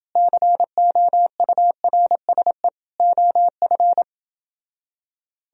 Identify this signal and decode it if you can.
Morse code, heard as a keyed tone, about 26 wpm, 700 Hz. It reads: COURSE OF